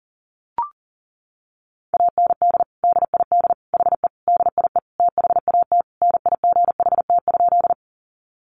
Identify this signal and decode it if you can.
Morse code, heard as a keyed tone, about 40 wpm, 700 Hz. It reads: AND DID HE DIE THAT NIGHT?
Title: Sentences from Top 1300 Words 40wpm